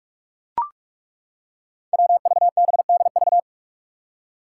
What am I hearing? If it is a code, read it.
Morse code, heard as a keyed tone, about 41 wpm, 700 Hz. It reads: WVBDV
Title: Five Challenging Letter-Number Combinations 45wpm